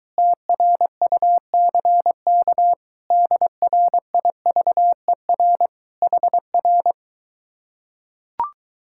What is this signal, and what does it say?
Morse code, heard as a keyed tone, about 22 wpm, 700 Hz. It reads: TRUCK DRIVER HR